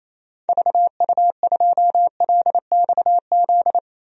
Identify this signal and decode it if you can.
Morse code, heard as a keyed tone, about 28 wpm, 700 Hz. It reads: VU2LXZ
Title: DXCC-100 Call Signs - 6 Characters - 28wpm